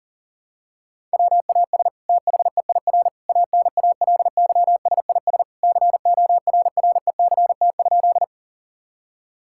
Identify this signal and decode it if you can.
Morse code, heard as a keyed tone, about 40 wpm, 700 Hz. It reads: WAS THEIR ANALYSIS CORRECT?
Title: Sentences from Top 1000 Words - Review Part 5 - 40wpm